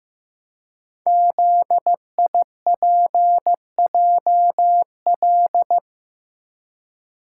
Morse code, heard as a keyed tone, about 15 wpm, 700 Hz. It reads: ZIPJL
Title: Code Groups - 5 Letter:Numbers - 15wpm